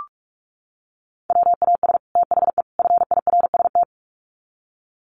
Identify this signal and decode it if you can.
Morse code, heard as a keyed tone, about 44 wpm, 700 Hz. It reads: WAS THE FIRST
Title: Sets of 3 Words 45wpm